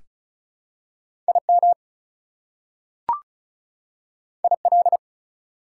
Morse code, heard as a keyed tone, about 36 wpm, 700 Hz. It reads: IM IL
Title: Prefixes and Suffixes 35wpm